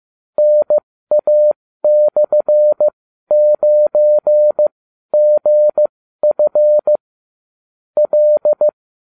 Morse code, heard as a keyed tone, about 15 wpm, spaced as for 12 wpm, 600 Hz. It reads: NA/9GF L